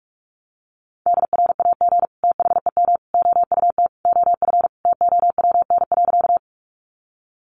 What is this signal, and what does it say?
Morse code, heard as a keyed tone, about 44 wpm, 700 Hz. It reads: DRAG THEM OUT OF TOWN.
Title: Sentences from Top 1500 Words 45wpm